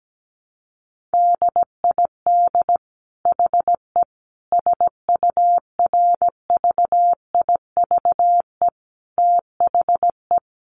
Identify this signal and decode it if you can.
Morse code, heard as a keyed tone, about 17 wpm, 700 Hz. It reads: DID HE SURVIVE THE
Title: Sentences from Top 1300 Words 17wpm